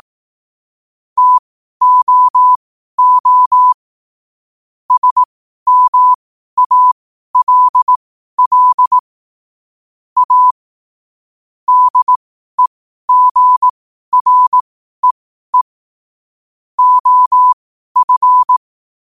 Morse code, heard as a keyed tone, about 18 wpm, spaced as for 12 wpm, 1000 Hz. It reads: TOO SMALL A DEGREE OF
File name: qotdfiles / 12wpm